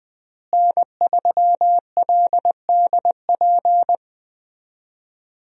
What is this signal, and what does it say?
Morse code, heard as a keyed tone, about 20 wpm, 700 Hz. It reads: N3LDP
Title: USA Call Signs - 5 Characters - Repeated 3x - 20wpm